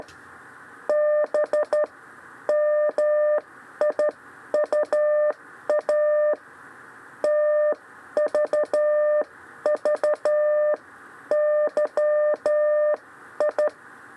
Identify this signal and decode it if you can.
Morse code, heard as a keyed tone, about 12 wpm, spaced as for 10 wpm, 600 Hz. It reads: BMIUA TVVYI
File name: 6 lezione cw 2